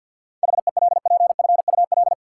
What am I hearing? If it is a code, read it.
Morse code, heard as a keyed tone, about 48 wpm, 700 Hz. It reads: HELPFUL
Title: Sentences from Top 1000 Words - Review Part 6 - New Word Drill - 50wpm